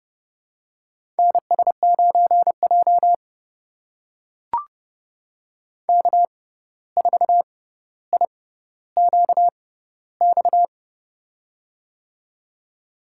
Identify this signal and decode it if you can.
Morse code, heard as a keyed tone, about 31 wpm, spaced as for 14 wpm, 700 Hz. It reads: NS9J K4IQX
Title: USA Call Signs - 6x Character Spacing - 30wpm